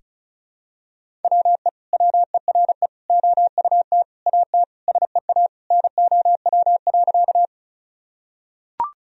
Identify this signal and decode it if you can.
Morse code, heard as a keyed tone, about 34 wpm, 700 Hz. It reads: WE WERE OUT AT SEA NOW.
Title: Sentences from Top 1000 Words - Review Part 1 - 35wpm